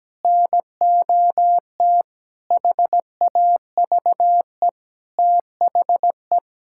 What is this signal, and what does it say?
Morse code, heard as a keyed tone, about 17 wpm, 700 Hz. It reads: NOT HAVE THE
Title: Sentences from Top 1600 Words 17wpm